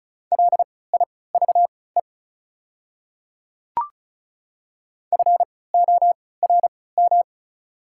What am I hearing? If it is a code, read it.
Morse code, heard as a keyed tone, about 34 wpm, spaced as for 20 wpm, 690 Hz. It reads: LIVE FORM
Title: 3x Character Spacing - Top 500 Words - 35wpm